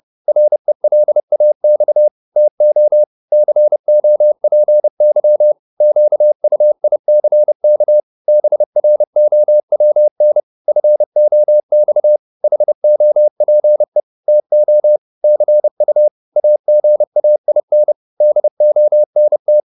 Morse code, heard as a keyed tone, about 30 wpm, 600 Hz. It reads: RELAX TO COPY QUICK BROWN FOX HOPE TO CU AGAIN DONT